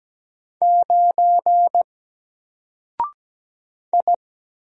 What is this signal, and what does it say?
Morse code, heard as a keyed tone, about 17 wpm, 700 Hz. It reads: 9 I
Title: Single Letter-Number Alt 17wpm